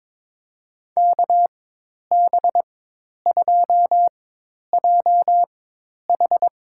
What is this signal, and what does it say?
Morse code, heard as a keyed tone, about 22 wpm, spaced as for 10 wpm, 700 Hz. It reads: KB2JH
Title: USA Call Signs - 4x Character Spacing - 22wpm